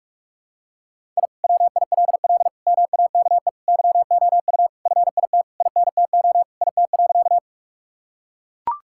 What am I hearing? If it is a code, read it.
Morse code, heard as a keyed tone, about 44 wpm, 700 Hz. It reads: I WILL MAKE YOU FIT INTO IT.
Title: Sentences from Top 1000 Words - Review Part 5 - 45wpm